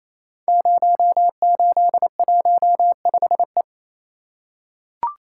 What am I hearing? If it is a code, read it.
Morse code, heard as a keyed tone, about 28 wpm, 700 Hz. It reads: 0815E
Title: Code Groups - 5 Letter:Numbers - 28wpm